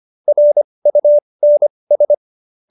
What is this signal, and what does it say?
Morse code, heard as a keyed tone, about 25 wpm, spaced as for 20 wpm, 590 Hz. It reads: RUNS
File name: qso220b 20